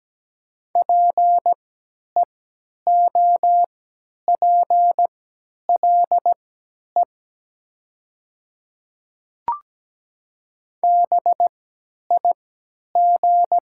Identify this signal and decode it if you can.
Morse code, heard as a keyed tone, about 17 wpm, spaced as for 10 wpm, 700 Hz. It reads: PEOPLE BIG
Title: 3x Character Spacing - Top 200 Words - 17wpm